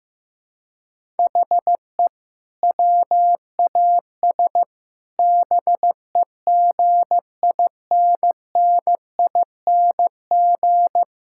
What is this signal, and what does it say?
Morse code, heard as a keyed tone, about 15 wpm, 700 Hz. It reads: HE WAS BEGINNING
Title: Sentences from Top 1000 Words - Review Part 5 - 15wpm